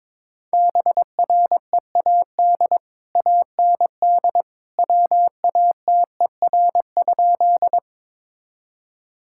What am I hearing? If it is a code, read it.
Morse code, heard as a keyed tone, about 21 wpm, 700 Hz. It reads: BREAD AND WATER?
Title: Sentences from Top 1600 Words 22wpm